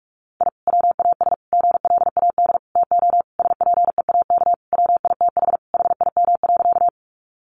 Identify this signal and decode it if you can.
Morse code, heard as a keyed tone, about 44 wpm, 700 Hz. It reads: I WAS GLAD TO SPEAK WITH HIM.
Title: Sentences from Top 1100 Words 45wpm